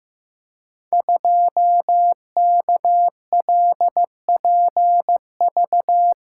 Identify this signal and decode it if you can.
Morse code, heard as a keyed tone, about 15 wpm, 700 Hz. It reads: 2KLPV
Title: Code Groups - 5 Letter:Numbers - 15wpm